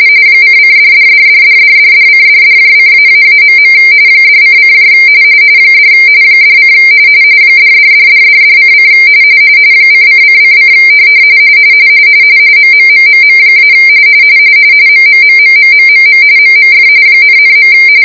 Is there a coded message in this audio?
Yes: RTTY from 0.4 to 17.7 s, about 45.45 baud. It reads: ::$$$$$$$$:$$:---.. $:: $ $: $$$$$$:: ::::: MMMMM- MMMMMMMM MMF ,,.